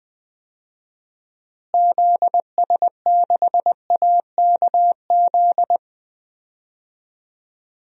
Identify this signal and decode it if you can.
Morse code, heard as a keyed tone, about 20 wpm, 700 Hz. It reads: ZS6AKZ
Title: DXCC-100 Call Signs - 6 Characters - Repeated 3x - 20wpm